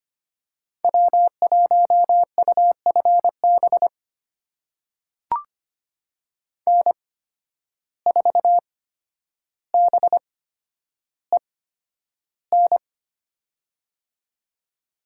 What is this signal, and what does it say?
Morse code, heard as a keyed tone, about 25 wpm, spaced as for 11 wpm, 700 Hz. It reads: W1UFB N4BEN
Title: USA Call Signs - 5 Characters - 8x Character Spacing - 25wpm